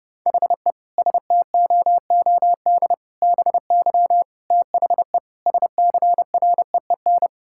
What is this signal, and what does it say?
Morse code, heard as a keyed tone, about 30 wpm, 700 Hz. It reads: HE STOOD BY THE SCREEN
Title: Sentences from Top 1000 Words - Review Part 6 - 30wpm